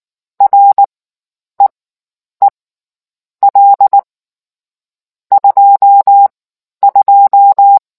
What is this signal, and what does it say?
Morse code, heard as a keyed tone, about 19 wpm, spaced as for 9 wpm, 800 Hz. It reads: REEL 22